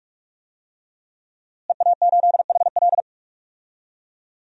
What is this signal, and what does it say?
Morse code, heard as a keyed tone, about 46 wpm, 700 Hz. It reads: EA8HL
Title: DXCC-100 Call Signs - 5 Characters - Repeated 2x - 45wpm